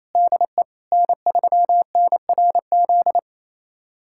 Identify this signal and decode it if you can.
Morse code, heard as a keyed tone, about 27 wpm, 700 Hz. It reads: DE N3NRZ
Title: USA Callsign DE Callsign - 5 Characters - 28wpm